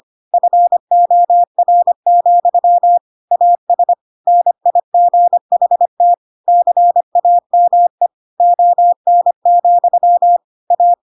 Morse code, heard as a keyed tone, about 25 wpm, 700 Hz. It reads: FOR, AS NIGHT CAME ON, A
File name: Pinocchio / Pinocchio ch5 25 25 ik1hge